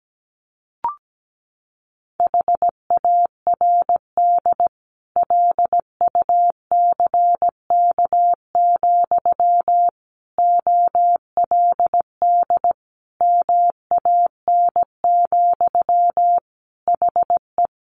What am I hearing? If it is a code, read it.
Morse code, heard as a keyed tone, about 17 wpm, 700 Hz. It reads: HARD LUCK, OLD MAN, HE
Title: Sentences from Top 1500 Words 17wpm